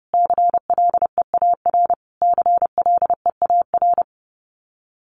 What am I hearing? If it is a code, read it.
Morse code, heard as a keyed tone, about 30 wpm, 700 Hz. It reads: CLEAR CLEAR
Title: Top 500 Words Repeated 30wpm